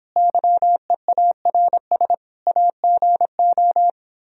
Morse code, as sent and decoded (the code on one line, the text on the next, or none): -.-- . .- .-. ... / .- --. ---
YEARS AGO